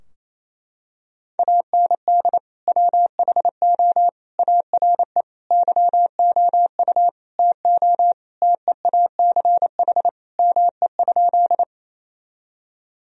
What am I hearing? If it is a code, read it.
Morse code, heard as a keyed tone, about 27 wpm, 690 Hz. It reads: AND WHO ARE YOU TO TEACH ME?
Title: Sentences from Top 1000 Words - Review Part 2 - 28wpm